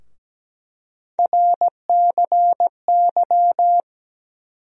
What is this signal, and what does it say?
Morse code, heard as a keyed tone, about 17 wpm, 700 Hz. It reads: RCY
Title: USA Call Sign Postfixes - 3 Letters 17wpm